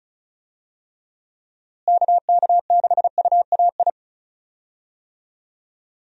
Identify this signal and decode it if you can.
Morse code, heard as a keyed tone, about 36 wpm, 700 Hz. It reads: KK6UAI
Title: USA Call Signs - 6 Characters - Repeated 3x - 35wpm